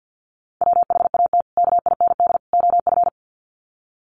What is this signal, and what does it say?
Morse code, heard as a keyed tone, about 48 wpm, 700 Hz. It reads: WHAT KIND OF
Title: Sets of 3 Words 50wpm